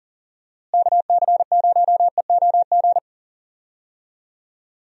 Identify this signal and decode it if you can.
Morse code, heard as a keyed tone, about 40 wpm, 700 Hz. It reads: KC0EOG
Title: USA Call Signs - 6 Characters - Repeated 3x - 40wpm